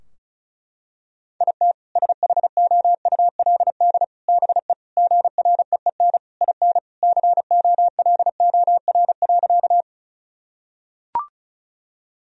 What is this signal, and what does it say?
Morse code, heard as a keyed tone, about 34 wpm, 700 Hz. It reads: IT SHOULD BE GREEN IN COLOR.
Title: Sentences from Top 1000 Words - Review Part 2 - 35wpm